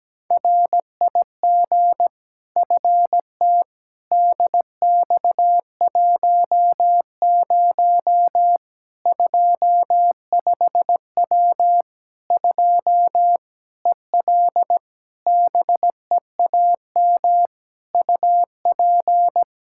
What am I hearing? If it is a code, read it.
Morse code, heard as a keyed tone, about 17 wpm, 700 Hz. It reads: RIG FT DX10 25W 2 EL BEAM UP
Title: Practice QSO - Exchange 2 - Info - 17wpm